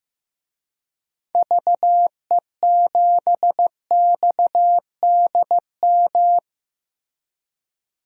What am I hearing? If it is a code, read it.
Morse code, heard as a keyed tone, about 16 wpm, 700 Hz. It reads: VE7XDM
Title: DXCC-100 Call Signs - 6 Characters - Repeated 3x - 15wpm